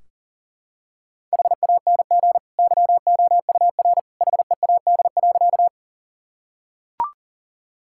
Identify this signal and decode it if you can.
Morse code, heard as a keyed tone, about 40 wpm, 700 Hz. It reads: HANG YOUR HEAD.
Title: Sentences from Top 1600 Words 40wpm